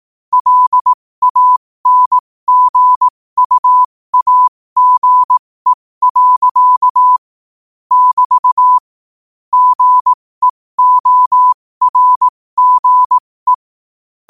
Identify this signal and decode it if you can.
Morse code, heard as a keyed tone, about 18 wpm, spaced as for 15 wpm, 1000 Hz. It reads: LANGUAGE. <BT> GEORGE
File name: qotdfiles / 15wpm